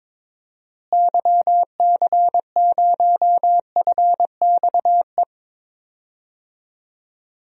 Morse code, heard as a keyed tone, about 22 wpm, 700 Hz. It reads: YC0FXE
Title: DXCC-100 Call Signs - 6 Characters - Repeated 2x - 22wpm